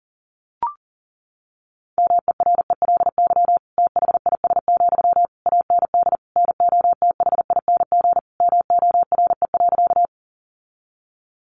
Morse code, heard as a keyed tone, about 40 wpm, 700 Hz. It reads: MERELY THIS, AND NOTHING MORE.
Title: Sentences from Top 1400 Words 40wpm